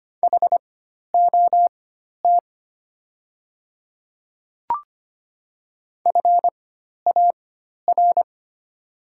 Morse code, heard as a keyed tone, about 25 wpm, spaced as for 12 wpm, 700 Hz. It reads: HOT FAR
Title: 4x Character Spacing - Top 300 Words - 25wpm